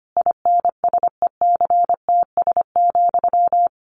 Morse code, heard as a keyed tone, about 25 wpm, 700 Hz. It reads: INSECTS,
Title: A Sound Of Thunder 25wpm